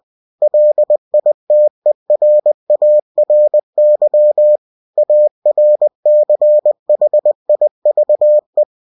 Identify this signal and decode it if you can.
Morse code, heard as a keyed tone, about 20 wpm, 600 Hz. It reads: LITERARY ARCHIVE
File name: cw-books / winnie-the-pooh / winnie-20wpm-page-0117